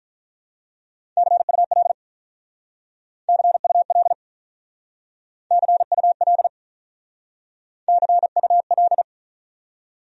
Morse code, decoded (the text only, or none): CUL CUL CUL CUL